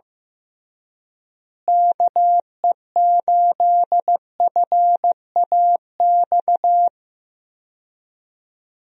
Morse code, heard as a keyed tone, about 15 wpm, 700 Hz. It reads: KE8FAX
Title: USA Call Signs - 6 Characters - Repeated 3x - 15wpm